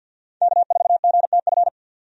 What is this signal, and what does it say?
Morse code, heard as a keyed tone, about 52 wpm, 700 Hz. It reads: K4GTF
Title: USA Call Signs - 5 Characters - 50wpm